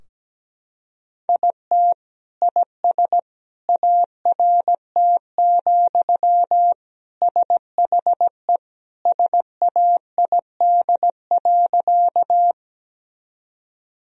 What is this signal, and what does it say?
Morse code, heard as a keyed tone, about 17 wpm, 700 Hz. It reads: IT IS ART, SHE SAID.
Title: Sentences from Top 1000 Words - Review Part 3 - 17wpm